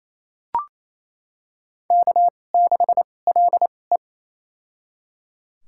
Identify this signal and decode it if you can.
Morse code, heard as a keyed tone, about 28 wpm, spaced as for 20 wpm, 700 Hz. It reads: K6LE
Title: USA Call Signs - 2x Character Spacing - 28wpm